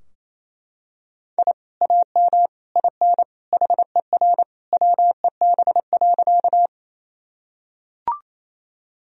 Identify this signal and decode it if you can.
Morse code, heard as a keyed tone, about 28 wpm, 700 Hz. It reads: I AM IN HER WEB.
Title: Sentences from Top 1000 Words - Review Part 4 - 28wpm